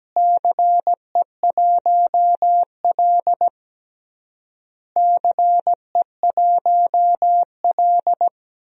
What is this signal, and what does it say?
Morse code, heard as a keyed tone, about 17 wpm, 700 Hz. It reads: CE1L CE1L
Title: DXCC-100 Call Signs - 4 Characters - Repeated 3x - 17wpm